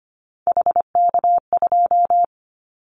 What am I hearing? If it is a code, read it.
Morse code, heard as a keyed tone, about 25 wpm, 700 Hz. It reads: HK2
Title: Callsign Prefixes for Top 100 DXCC - 25wpm